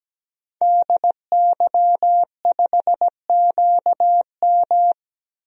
Morse code, heard as a keyed tone, about 17 wpm, 700 Hz. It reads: DY5QM
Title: DXCC-100 Call Signs - 5 Characters - 17wpm